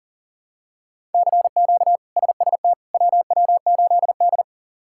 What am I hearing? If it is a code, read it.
Morse code, heard as a keyed tone, about 40 wpm, 700 Hz. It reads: CQ SST WW8D
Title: K1USN SST QSO Practice - 40wpm